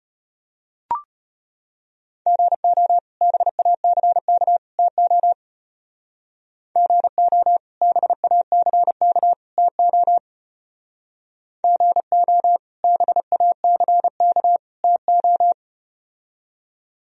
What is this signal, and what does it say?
Morse code, heard as a keyed tone, about 33 wpm, 700 Hz. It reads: GO BACK TO GO BACK TO GO BACK TO